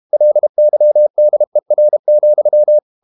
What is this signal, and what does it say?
Morse code, heard as a keyed tone, about 32 wpm, 600 Hz. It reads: LYDER,
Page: Ferie 32 wpm